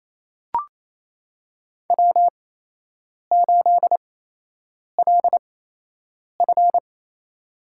Morse code, heard as a keyed tone, about 28 wpm, spaced as for 8 wpm, 700 Hz. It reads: W8LF
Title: USA Call Signs - 4 Characters - 8x Character Spacing - 28wpm